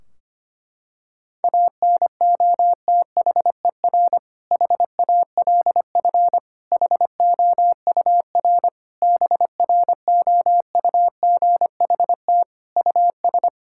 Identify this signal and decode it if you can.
Morse code, heard as a keyed tone, about 25 wpm, 700 Hz. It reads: ANOTHER HALF HOUR BROUGHT US
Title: Sentences from Top 1600 Words 25wpm